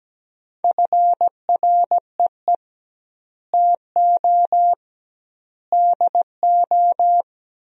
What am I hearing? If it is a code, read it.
Morse code, heard as a keyed tone, about 17 wpm, 700 Hz. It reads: FREE TO DO
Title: 2x Word-Spacing - Sentences from Top 500 Words - 17wpm